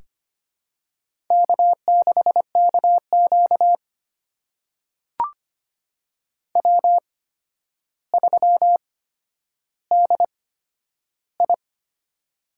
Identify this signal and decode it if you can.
Morse code, heard as a keyed tone, about 25 wpm, spaced as for 11 wpm, 700 Hz. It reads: K6KQ W3DI
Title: USA Call Signs - 8x Character Spacing - 25wpm